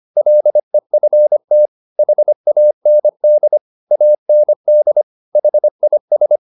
Morse code, heard as a keyed tone, about 25 wpm, 600 Hz. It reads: LEFT HAND AND HIS